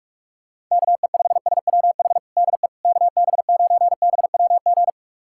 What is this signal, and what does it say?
Morse code, heard as a keyed tone, about 46 wpm, 690 Hz. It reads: KE5SWH DE KB9BWG
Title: USA Callsign DE Callsign - 6 Characters - 45wpm